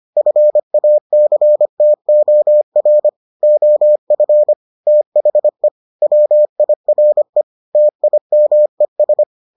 Morse code, heard as a keyed tone, about 25 wpm, 600 Hz. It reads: FACTOR OF THE WIRE TIMES